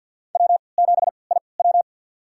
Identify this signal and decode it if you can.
Morse code, heard as a keyed tone, about 52 wpm, spaced as for 28 wpm, 700 Hz. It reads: W7IW
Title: USA Call Signs - 3x Character Spacing - 50wpm